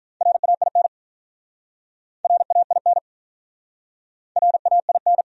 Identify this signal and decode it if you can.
Morse code, heard as a keyed tone, about 47 wpm, 700 Hz. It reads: RAIN RAIN RAIN